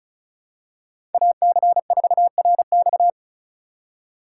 Morse code, heard as a keyed tone, about 34 wpm, 700 Hz. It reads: AC4RX
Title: USA Call Signs - 5 Characters - 3x Character Spacing - 35wpm